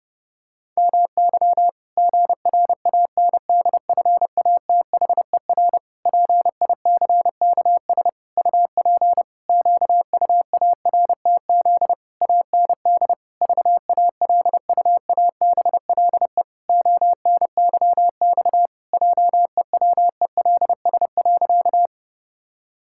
Morse code, heard as a keyed tone, about 30 wpm, 700 Hz. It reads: MY GRANDFATHER PICKS UP QUARTZ AND VALUABLE ONYX JEWELS.
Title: Pangram Challenge v2 - 30wpm